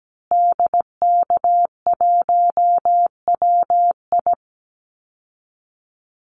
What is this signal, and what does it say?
Morse code, heard as a keyed tone, about 17 wpm, 700 Hz. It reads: DK1WI
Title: Call Signs Repeated 2x - 17wpm